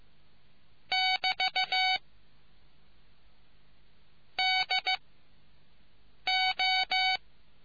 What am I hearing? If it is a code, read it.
Morse code, heard as a keyed tone, about 15 wpm, 750 Hz. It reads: <BT> D O